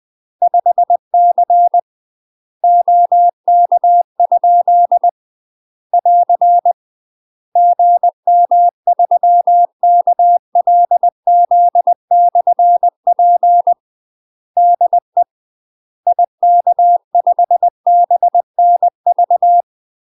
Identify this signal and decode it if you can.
Morse code, heard as a keyed tone, about 20 wpm, 700 Hz. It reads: 5C OK? <AR> GM3KLZ/P DE IK5BNV
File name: UK MASTER QSOs TRAINING / 20WPM / UK MASTER QSOs 0067